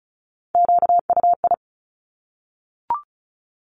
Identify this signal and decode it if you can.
Morse code, heard as a keyed tone, about 36 wpm, 700 Hz. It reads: QUI
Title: Top Three Letter Combinations in Words 35wpm